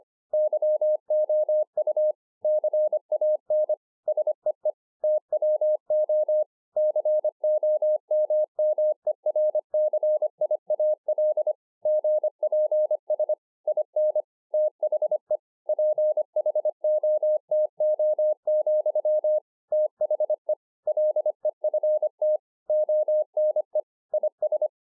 Morse code, heard as a keyed tone, about 25 wpm, 610 Hz. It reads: YOU CAN SEE TWO COMMERCIAL GPS IN THE PHOTO, THE LEFT ONE IS